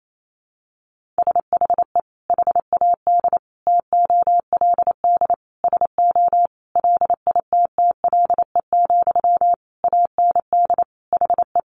Morse code, heard as a keyed tone, about 28 wpm, 700 Hz. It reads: SHE HAD TOLD SO LITTLE, AND HE
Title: Sentences from Top 1500 Words 28wpm